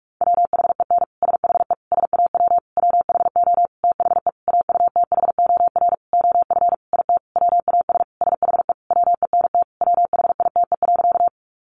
Morse code, heard as a keyed tone, about 44 wpm, 700 Hz. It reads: WHEN SHE SAW WHO THE AUTHOR OF IT WAS SHE WENT WHITE.
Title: Sentences from Top 1200 Words 45wpm